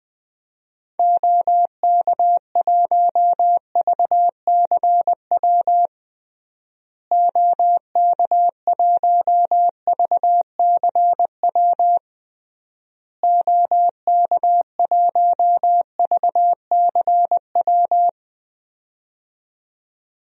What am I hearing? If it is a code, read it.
Morse code, heard as a keyed tone, about 20 wpm, 700 Hz. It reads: OK1VCW OK1VCW OK1VCW
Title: DXCC-100 Call Signs - 6 Characters - Repeated 3x - 20wpm